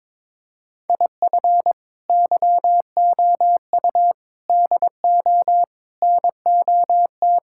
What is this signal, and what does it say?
Morse code, heard as a keyed tone, about 22 wpm, 700 Hz. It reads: IF YOU DO NOT
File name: practice / 22wpm / Build A Phrase - 5-Word Phrases - 22wpm